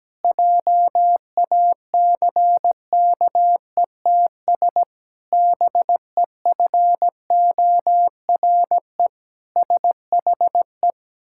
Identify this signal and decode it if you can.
Morse code, heard as a keyed tone, about 17 wpm, 700 Hz. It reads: JACKETS BEFORE SHE
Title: Pangram Challenge v2 - 17wpm